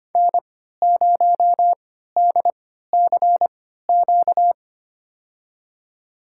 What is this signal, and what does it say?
Morse code, heard as a keyed tone, about 25 wpm, spaced as for 14 wpm, 700 Hz. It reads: N0DCQ
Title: USA Call Signs - 3x Character Spacing - 25wpm